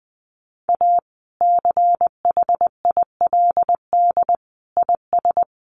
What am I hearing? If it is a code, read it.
Morse code, heard as a keyed tone, about 20 wpm, 700 Hz. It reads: A CHILD IS
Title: Sentences from Top 1400 Words 20wpm